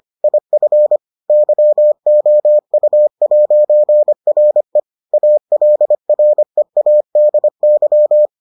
Morse code, heard as a keyed tone, about 25 wpm, 600 Hz. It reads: IF YOU'RE ALREADY